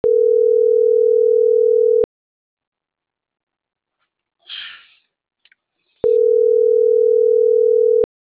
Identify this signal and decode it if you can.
Ringback tone (North America), 440 + 480 Hz.
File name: CENTRAL MAERSK / out polanco.msk